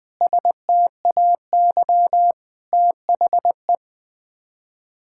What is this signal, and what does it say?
Morse code, heard as a keyed tone, about 20 wpm, 700 Hz. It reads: STAY THE
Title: Sets of 9 Words - 20wpm